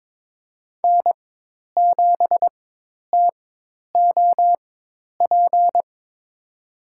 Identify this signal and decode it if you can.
Morse code, heard as a keyed tone, about 22 wpm, spaced as for 10 wpm, 700 Hz. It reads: N7TOP
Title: USA Call Signs - 4x Character Spacing - 22wpm